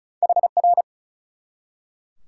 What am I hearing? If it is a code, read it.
Morse code, heard as a keyed tone, about 47 wpm, spaced as for 42 wpm, 700 Hz. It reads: HR
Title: Speed-Racing Top 90 QSO Elements 35wpm